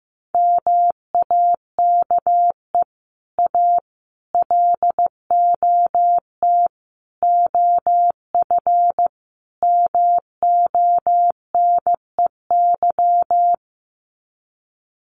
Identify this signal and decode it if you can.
Morse code, heard as a keyed tone, about 15 wpm, 700 Hz. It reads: MAKE A LOT OF MONEY
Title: Sets of 5 Words Encore2 15wpm